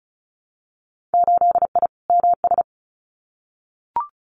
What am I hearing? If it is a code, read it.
Morse code, heard as a keyed tone, about 34 wpm, 700 Hz. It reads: 8I MS
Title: Winter Field Day Exchanges Encore 35wpm